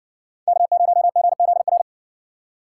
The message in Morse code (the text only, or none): K0GZR